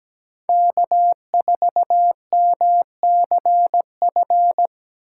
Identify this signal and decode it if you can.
Morse code, heard as a keyed tone, about 17 wpm, 700 Hz. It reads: K4MCF
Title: USA Call Signs - 5 Characters - 2x Character Spacing - 17wpm